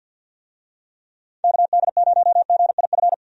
Morse code, heard as a keyed tone, about 51 wpm, 700 Hz. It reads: KD0GIF
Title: USA Call Signs - 6 Characters - 50wpm